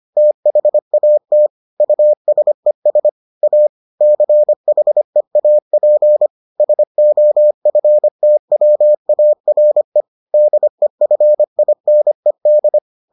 Morse code, heard as a keyed tone, about 25 wpm, 600 Hz. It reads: THAT USES A CHEAP SOFTWARE DEFINED